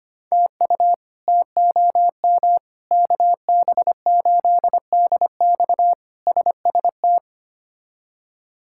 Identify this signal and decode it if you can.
Morse code, heard as a keyed tone, about 25 wpm, 700 Hz. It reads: TU TOM KB8DX SST
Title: K1USN SST QSO Practice - 25wpm